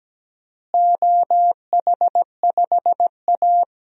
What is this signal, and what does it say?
Morse code, heard as a keyed tone, about 17 wpm, 700 Hz. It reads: OH5A
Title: DXCC-100 Call Signs - 4 Characters - Repeated 3x - 17wpm